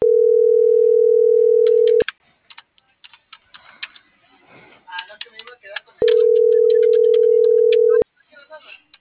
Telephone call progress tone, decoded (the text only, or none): ringback tone